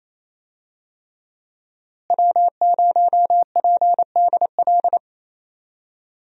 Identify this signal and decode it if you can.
Morse code, heard as a keyed tone, about 28 wpm, 700 Hz. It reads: W0PDL
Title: USA Call Signs - 5 Characters - Repeated 3x - 28wpm